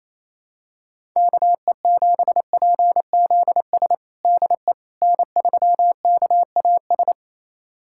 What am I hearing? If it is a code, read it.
Morse code, heard as a keyed tone, about 28 wpm, 700 Hz. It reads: KE7PZS DE N3KAS
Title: USA Callsign DE Callsign 28wpm